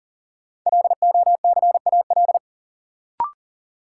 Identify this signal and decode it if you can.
Morse code, heard as a keyed tone, about 40 wpm, 700 Hz. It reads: LOCAL
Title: Sentences from Top 1500 Words - New Word Drill Alt - 40wpm